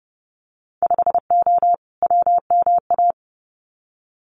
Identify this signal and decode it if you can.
Morse code, heard as a keyed tone, about 30 wpm, 700 Hz. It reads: 5O WMA
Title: Winter Field Day Exchanges 30wpm